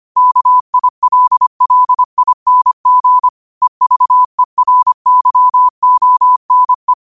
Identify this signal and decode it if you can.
Morse code, heard as a keyed tone, about 25 wpm, 1000 Hz. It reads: KILLING EVERYONE
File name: qotdfiles / 25wpm